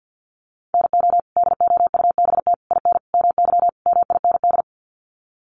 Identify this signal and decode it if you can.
Morse code, heard as a keyed tone, about 50 wpm, 700 Hz. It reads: NO DOUBT IN MY MIND
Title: Sets of 5 Words Encore2 50wpm